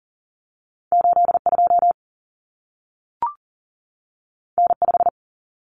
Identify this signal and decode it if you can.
Morse code, heard as a keyed tone, about 40 wpm, 700 Hz. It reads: 82 N5
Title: Common Cut Numbers - 2-Digit 40wpm